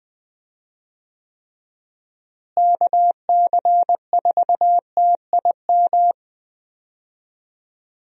Morse code, heard as a keyed tone, about 20 wpm, 700 Hz. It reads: KC4TIM
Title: USA Call Signs - 6 Characters - Repeated 3x - 20wpm